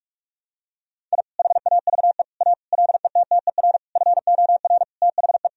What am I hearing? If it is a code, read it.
Morse code, heard as a keyed tone, about 41 wpm, 700 Hz. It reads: I HAVE A LETTER FOR THE
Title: Sentences from Top 1000 Words - Review Part 1 - 45wpm